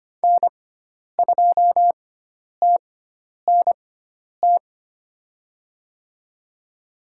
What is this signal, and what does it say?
Morse code, heard as a keyed tone, about 25 wpm, spaced as for 10 wpm, 700 Hz. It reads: N2TNT
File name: Lesson 16g - Letter c - Call Signs 25wpm